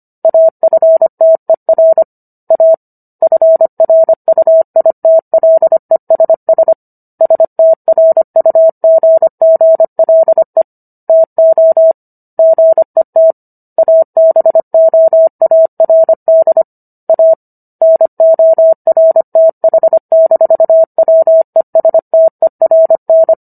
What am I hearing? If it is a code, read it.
Morse code, heard as a keyed tone, about 25 wpm, 650 Hz. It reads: AFTER A FRUITLESS STRUGGLE TO GET ABOARD A NORTH-WESTERN